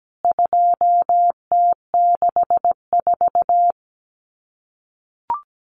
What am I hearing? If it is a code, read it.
Morse code, heard as a keyed tone, about 17 wpm, 700 Hz. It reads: 2T64
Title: CWT Exchange 17wpm